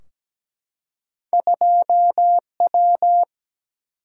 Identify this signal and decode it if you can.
Morse code, heard as a keyed tone, about 17 wpm, 700 Hz. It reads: 2W
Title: Single-Letter Single-Number 17wpm